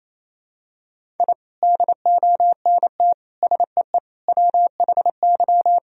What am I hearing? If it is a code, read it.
Morse code, heard as a keyed tone, about 28 wpm, 700 Hz. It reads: I DONT SEE WHY
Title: Sentences from Top 1000 Words - Review Part 2 - 28wpm